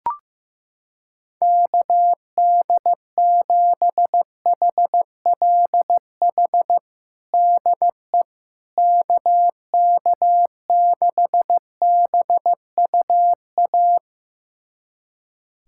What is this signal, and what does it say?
Morse code, heard as a keyed tone, about 16 wpm, 700 Hz. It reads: KD7HLH DE KK6BUA
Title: USA Callsign DE Callsign - 6 Characters - 15wpm